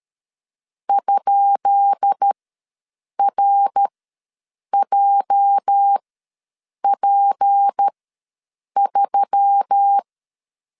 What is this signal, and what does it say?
Morse code, heard as a keyed tone, about 13 wpm, spaced as for 7 wpm, 800 Hz. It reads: ?RJP3
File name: LEZIONI / 10 SIGNS / 10B INT ws 12wpm 3527